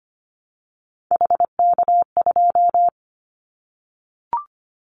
Morse code, heard as a keyed tone, about 25 wpm, 700 Hz. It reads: HK2
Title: Callsign Prefixes for Top 100 DXCC - 25wpm